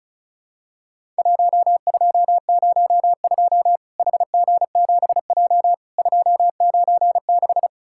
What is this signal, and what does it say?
Morse code, heard as a keyed tone, about 34 wpm, 700 Hz. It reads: 1202 HG7J 296
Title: Worked All Europe DX Contest - QTC Exchange - 35wpm